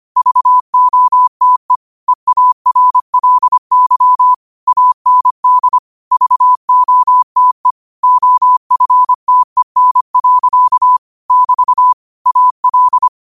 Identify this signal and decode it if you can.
Morse code, heard as a keyed tone, about 25 wpm, 1000 Hz. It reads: VOTE EARLY AND VOTE OFTEN. <BT> AL
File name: qotdfiles / 25wpm